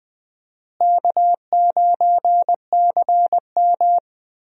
Morse code, heard as a keyed tone, about 20 wpm, 700 Hz. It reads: K9CM
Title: USA Call Signs - 4 Characters - Repeated 2x - 20wpm